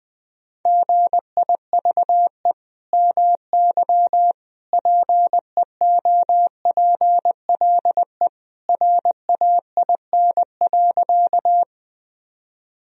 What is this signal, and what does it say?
Morse code, heard as a keyed tone, about 20 wpm, 700 Hz. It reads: GIVE MY PEOPLE RAIN.
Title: Sentences from Top 1000 Words - Review Part 1 - 20wpm